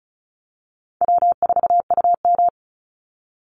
Morse code, heard as a keyed tone, about 34 wpm, 700 Hz. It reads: W4UM
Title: Easy Call Signs Repeated 2x - 35wpm